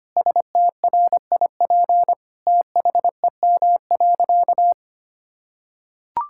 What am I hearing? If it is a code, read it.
Morse code, heard as a keyed tone, about 25 wpm, 700 Hz. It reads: STRIP THEM.
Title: Sentences from Top 1600 Words 25wpm